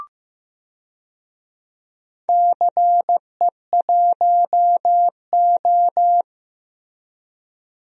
Morse code, heard as a keyed tone, about 16 wpm, 700 Hz. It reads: CE1O
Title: DXCC-100 Call Signs - 4 Characters - Repeated 3x - 15wpm